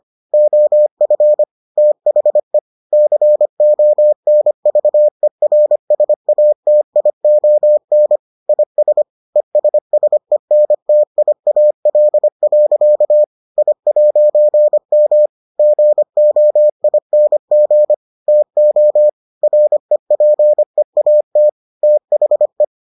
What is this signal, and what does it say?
Morse code, heard as a keyed tone, about 25 wpm, 600 Hz. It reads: OF THE CONVERSATION IS ESSENTIAL. I'M GOING TO REPEAT THE